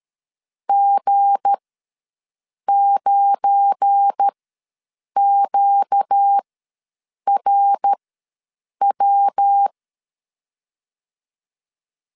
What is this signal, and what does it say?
Morse code, heard as a keyed tone, about 13 wpm, spaced as for 7 wpm, 800 Hz. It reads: G9QRW
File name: 7 GF9Z ws 12wpm